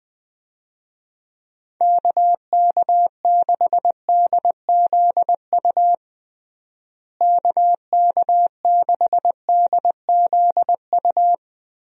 Morse code, heard as a keyed tone, about 20 wpm, 700 Hz. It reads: KK6DZU KK6DZU
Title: USA Call Signs - 6 Characters - Repeated 3x - 20wpm